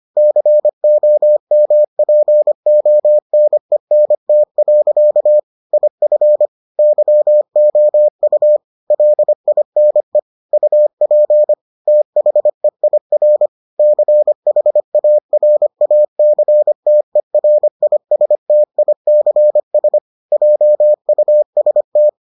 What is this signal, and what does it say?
Morse code, heard as a keyed tone, about 25 wpm, 600 Hz. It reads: COMPONENT. IF YOU LINE UP THEIR CHARACTERISTICS JUST